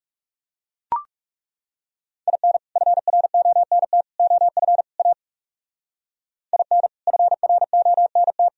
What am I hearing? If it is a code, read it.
Morse code, heard as a keyed tone, about 37 wpm, 700 Hz. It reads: IN FRONT OF A IN FRONT
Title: Speed-Racing - Sets of 4 Words 30wpm